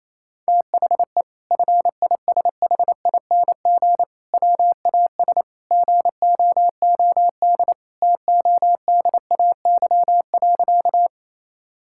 Morse code, heard as a keyed tone, about 27 wpm, 700 Hz. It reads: THE FISHING WAS GOOD TODAY.
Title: Sentences from Top 1000 Words - Review Part 5 - 28wpm